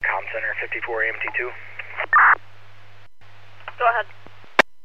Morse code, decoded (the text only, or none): none